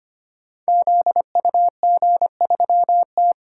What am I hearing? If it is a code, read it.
Morse code, heard as a keyed tone, about 25 wpm, 700 Hz. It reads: ZUG3T
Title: Code Groups - 5 Letter:Numbers - 25wpm